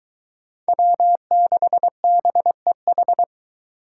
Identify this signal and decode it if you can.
Morse code, heard as a keyed tone, about 22 wpm, 700 Hz. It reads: W6BEH